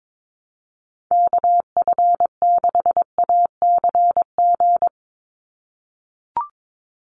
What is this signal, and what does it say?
Morse code, heard as a keyed tone, about 22 wpm, 700 Hz. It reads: KF6ACG
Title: USA Call Signs 22wpm